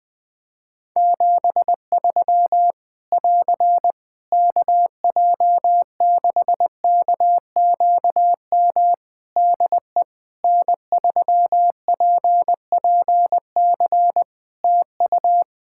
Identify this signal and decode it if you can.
Morse code, heard as a keyed tone, about 20 wpm, 700 Hz. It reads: 73 <AR> KJ6KQM DE N3PPC TU
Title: K1USN SST QSO Practice 20wpm